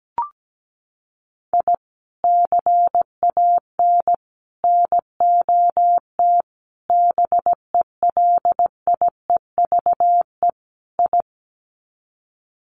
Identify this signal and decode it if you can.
Morse code, heard as a keyed tone, about 17 wpm, 700 Hz. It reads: I CAN NOT BELIEVE I
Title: Sets of 5 Words Encore2 17wpm